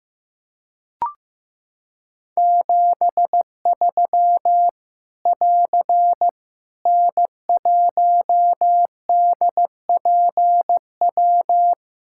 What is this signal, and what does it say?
Morse code, heard as a keyed tone, about 15 wpm, 700 Hz. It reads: 73 <AR> N1DPW
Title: K1USN SST QSO Practice 15wpm